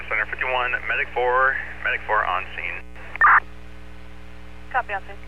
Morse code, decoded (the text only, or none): none